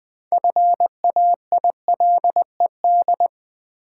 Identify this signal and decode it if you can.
Morse code, heard as a keyed tone, about 20 wpm, 700 Hz. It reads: FAILED
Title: Build A Word - 2 Letters At A Time - 6-Letter Words - 20wpm